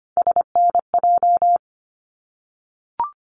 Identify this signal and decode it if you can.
Morse code, heard as a keyed tone, about 25 wpm, 700 Hz. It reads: SNJ
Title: ARRL Sections 25wpm